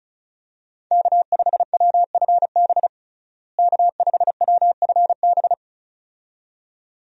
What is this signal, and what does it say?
Morse code, heard as a keyed tone, about 35 wpm, 700 Hz. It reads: K5WFB K5WFB
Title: USA Call Signs - 5 Characters - Repeated 2x - 35wpm